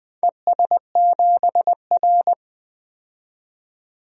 Morse code, heard as a keyed tone, about 20 wpm, 700 Hz. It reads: ES7R
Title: DXCC-100 Call Signs - 4 Characters - 20wpm